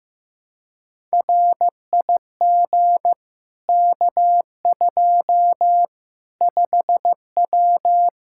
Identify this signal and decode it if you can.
Morse code, heard as a keyed tone, about 15 wpm, 700 Hz. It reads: RIG K2 5W
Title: Practice QSO - Exchange 2 - Info - 15wpm